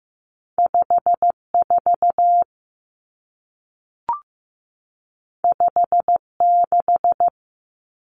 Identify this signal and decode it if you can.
Morse code, heard as a keyed tone, about 15 wpm, 700 Hz. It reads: 5 4 5 6
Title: Common Cut Numbers - 2-Digit 15wpm